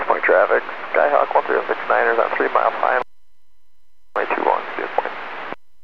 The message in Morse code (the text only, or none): none